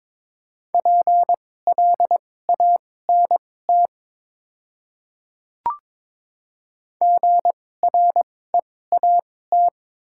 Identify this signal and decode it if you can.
Morse code, heard as a keyed tone, about 21 wpm, spaced as for 16 wpm, 690 Hz. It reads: PLANT GREAT
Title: 2x Character Spacing - Top 300 Words - 22wpm